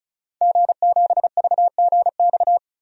Morse code, heard as a keyed tone, about 34 wpm, 700 Hz. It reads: G7VGX
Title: DXCC-100 Call Signs - 5 Characters - 35wpm